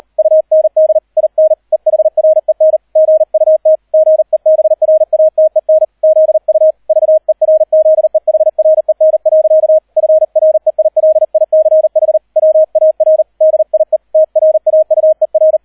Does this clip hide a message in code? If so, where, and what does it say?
Yes: Morse code from 0.2 to 15.6 s, heard as a keyed tone, about 38 wpm, 630 Hz. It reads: UND IN EHREN GUT GEBRATEN ZU VERZEHREN. FREILICH WAR DIE TRAUER